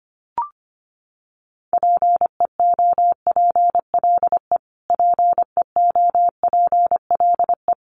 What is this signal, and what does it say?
Morse code, heard as a keyed tone, about 25 wpm, 700 Hz. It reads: PEOPLE PEOPLE
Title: Top 500 Words Repeated 25wpm